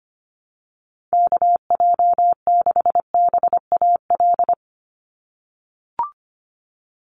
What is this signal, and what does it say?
Morse code, heard as a keyed tone, about 25 wpm, 700 Hz. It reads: KJ6BAL
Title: USA Call Signs 25wpm